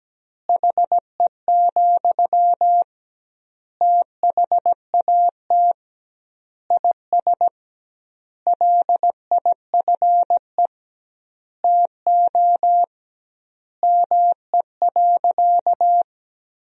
Morse code, heard as a keyed tone, about 17 wpm, 700 Hz. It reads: HE, THAT IS LIFE TO ME.
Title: 2x Word-Spacing - Sentences from Top 500 Words - 17wpm